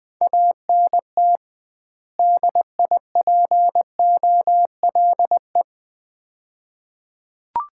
Morse code, heard as a keyed tone, about 20 wpm, 700 Hz. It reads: ANT DIPOLE
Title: Common QSO Phrases - 2x Word-Spacing - 20wpm